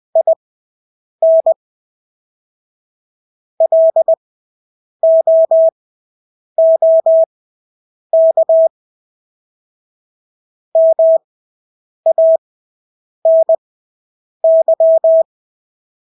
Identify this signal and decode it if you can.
Morse code, heard as a keyed tone, about 20 wpm, spaced as for 8 wpm, 650 Hz. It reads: IN LOOK MANY
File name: Engelske ord 20/8 wpm